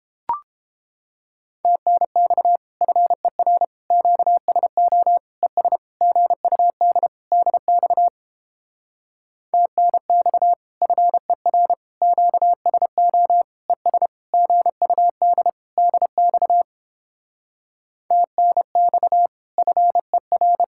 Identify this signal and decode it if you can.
Morse code, heard as a keyed tone, about 29 wpm, 700 Hz. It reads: TNX FER QSO ES GUD DX TNX FER QSO ES GUD DX TNX FER